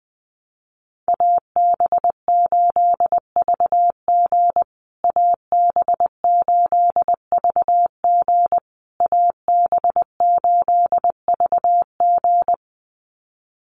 Morse code, heard as a keyed tone, about 20 wpm, 700 Hz. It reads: AB8VG AB8VG AB8VG
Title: USA Call Signs Repeated 3x 20wpm